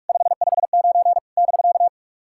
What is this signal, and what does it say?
Morse code, heard as a keyed tone, about 44 wpm, 690 Hz. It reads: 559 <BK>
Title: SOTA QSO Practice 45wpm